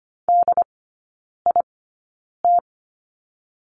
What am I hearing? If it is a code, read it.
Morse code, heard as a keyed tone, about 25 wpm, spaced as for 9 wpm, 690 Hz. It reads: DIT
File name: Lesson 14e - Number 2 - Three Characters 25wpm